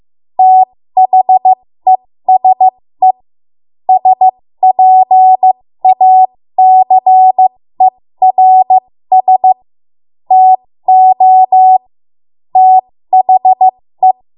Morse code, heard as a keyed tone, about 15 wpm, spaced as for 13 wpm, 750 Hz. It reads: THESE SPACERS TO THE
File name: Morse / 13WPM